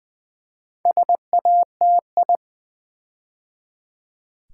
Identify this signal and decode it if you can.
Morse code, heard as a keyed tone, about 20 wpm, 700 Hz. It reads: SATI
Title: Build A Word - 4 Letters At A Time - 12-Letter Words - 20wpm